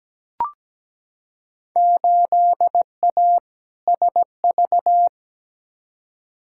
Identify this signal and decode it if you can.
Morse code, heard as a keyed tone, about 17 wpm, 700 Hz. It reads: 8A SV
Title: Field Day Exchanges Encore - 17wpm